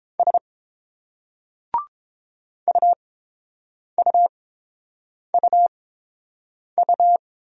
Morse code, heard as a keyed tone, about 30 wpm, spaced as for 7 wpm, 700 Hz. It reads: S UUUU